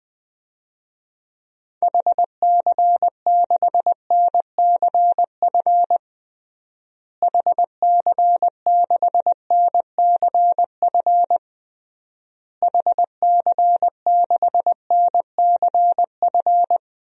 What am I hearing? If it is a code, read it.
Morse code, heard as a keyed tone, about 20 wpm, 700 Hz. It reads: HC6NCF HC6NCF HC6NCF
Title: DXCC-100 Call Signs - 6 Characters - Repeated 3x - 20wpm